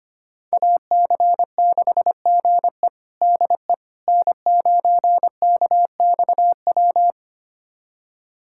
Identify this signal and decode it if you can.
Morse code, heard as a keyed tone, about 25 wpm, 700 Hz. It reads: AC6GE DE N9KXW
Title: USA Callsign DE Callsign - 5 Characters - 25wpm